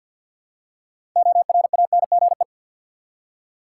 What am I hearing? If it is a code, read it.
Morse code, heard as a keyed tone, about 48 wpm, 700 Hz. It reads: ORANGE
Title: Six Letter Words - 50wpm